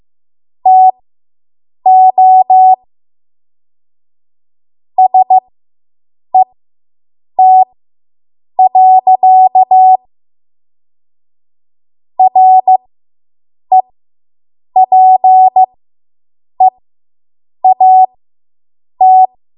Morse code, heard as a keyed tone, about 15 wpm, spaced as for 7 wpm, 750 Hz. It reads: TO SET. REPEAT